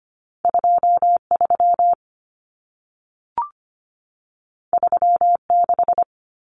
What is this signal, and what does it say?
Morse code, heard as a keyed tone, about 25 wpm, 700 Hz. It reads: 23 36
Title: Numbers - 2-Digit 25wpm